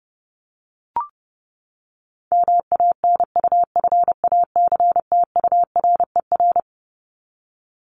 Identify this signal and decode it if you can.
Morse code, heard as a keyed tone, about 30 wpm, 700 Hz. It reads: MANUFACTURER
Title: Top 1751-2000 Words 30wpm